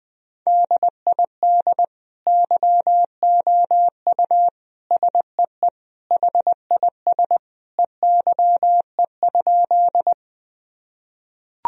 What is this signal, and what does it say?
Morse code, heard as a keyed tone, about 21 wpm, 700 Hz. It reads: DID YOU SEE HIS EYE?
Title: Sentences from Top 1000 Words - Review Part 1 - 20wpm